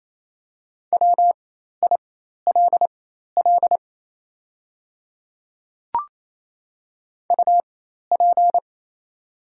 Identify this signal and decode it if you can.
Morse code, heard as a keyed tone, about 28 wpm, spaced as for 13 wpm, 700 Hz. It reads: WILL UP
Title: 4x Character Spacing - Top 200 Words - 28wpm